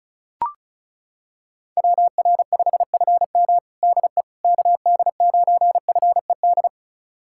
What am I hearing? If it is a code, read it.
Morse code, heard as a keyed tone, about 36 wpm, 700 Hz. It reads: WR5FM DE KD9FED
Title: USA Callsign DE Callsign 35wpm